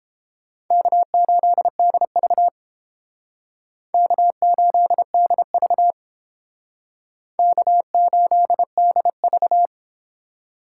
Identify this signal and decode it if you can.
Morse code, heard as a keyed tone, about 31 wpm, 700 Hz. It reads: K8DV K8DV K8DV